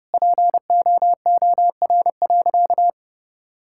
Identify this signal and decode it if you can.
Morse code, heard as a keyed tone, about 30 wpm, 700 Hz. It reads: POOR.
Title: Sentences from Top 1000 Words - Review Part 5 - 30wpm